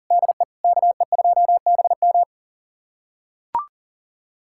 Morse code, heard as a keyed tone, about 40 wpm, 700 Hz. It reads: DE KE2BM
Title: USA Callsign DE Callsign - 5 Characters - 40wpm